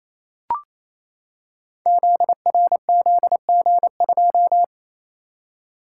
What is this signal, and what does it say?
Morse code, heard as a keyed tone, about 27 wpm, 700 Hz. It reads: ZRZG2
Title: Code Groups - 5 Letter:Numbers - 28wpm